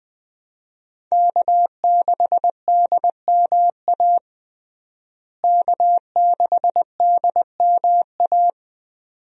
Morse code, heard as a keyed tone, about 20 wpm, 700 Hz. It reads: K6DMA K6DMA
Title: USA Call Signs - 5 Characters - Repeated 2x - 20wpm